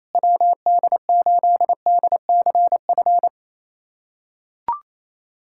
Morse code, heard as a keyed tone, about 28 wpm, 700 Hz. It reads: WD8DCF
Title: USA Callsign DE Callsign - 6 Characters - 28wpm